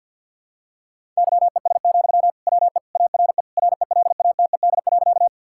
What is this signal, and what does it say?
Morse code, heard as a keyed tone, about 50 wpm, 700 Hz. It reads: YES, WE ARE RELATED.
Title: Sentences from Top 1000 Words - Review Part 5 - 50wpm